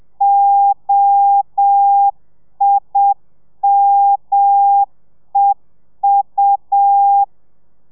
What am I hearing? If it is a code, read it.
Morse code, heard as a keyed tone, about 7 wpm, 800 Hz. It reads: OIMEU
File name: telegrafia / i2tfj / Lez5 30